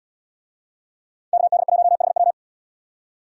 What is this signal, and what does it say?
Morse code, heard as a keyed tone, about 77 wpm, 700 Hz. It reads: KD1HW